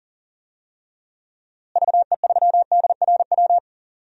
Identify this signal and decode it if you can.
Morse code, heard as a keyed tone, about 40 wpm, 700 Hz. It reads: VE3DRW
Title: DXCC-100 Call Signs - 6 Characters - Repeated 3x - 40wpm